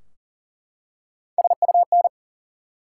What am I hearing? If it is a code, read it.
Morse code, heard as a keyed tone, about 41 wpm, 700 Hz. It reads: SUN
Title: Three Letter Words - 40wpm